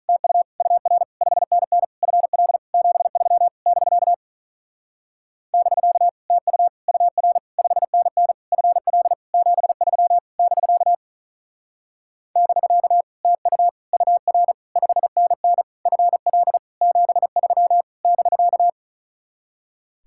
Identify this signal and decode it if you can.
Morse code, heard as a keyed tone, about 47 wpm, 700 Hz. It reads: TU UR 5NN FL 73 <BK> <BK> TU UR 5NN FL 73 <BK> <BK> TU UR 5NN FL 73 <BK>